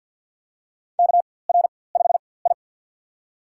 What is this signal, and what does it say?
Morse code, heard as a keyed tone, about 53 wpm, spaced as for 24 wpm, 700 Hz. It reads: KR5I
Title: USA Call Signs - 4x Character Spacing - 50wpm